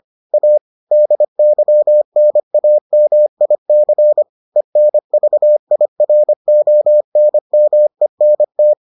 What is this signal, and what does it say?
Morse code, heard as a keyed tone, about 25 wpm, 600 Hz. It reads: A DYNAMIC ENVIRONMENT